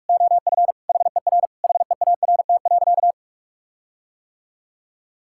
Morse code, heard as a keyed tone, about 46 wpm, 700 Hz. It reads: OF HER HEART.
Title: Sentences from Top 1600 Words 45wpm